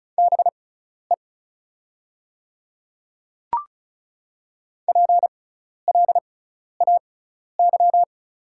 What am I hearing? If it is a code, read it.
Morse code, heard as a keyed tone, about 36 wpm, spaced as for 12 wpm, 690 Hz. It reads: BE PLAY
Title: 6x Character Spacing - Top 300 Words - 35wpm